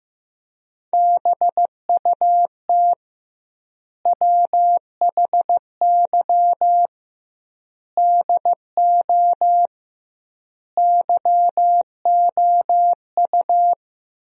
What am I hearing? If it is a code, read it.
Morse code, heard as a keyed tone, about 15 wpm, 700 Hz. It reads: BUT WHY DO YOU
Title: 2x Word-Spacing - Sentences from Top 200 Words - 15wpm